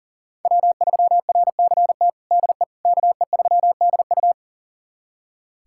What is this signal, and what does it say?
Morse code, heard as a keyed tone, about 40 wpm, 700 Hz. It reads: W3RCT DE KE3DU
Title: USA Callsign DE Callsign - 5 Characters - 40wpm